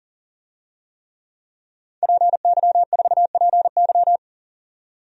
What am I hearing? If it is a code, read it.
Morse code, heard as a keyed tone, about 38 wpm, 700 Hz. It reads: PY4PY
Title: DXCC-100 Call Signs - 5 Characters - 40wpm